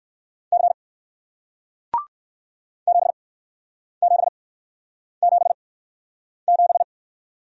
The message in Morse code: C 7777

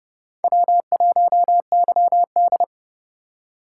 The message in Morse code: W1YD

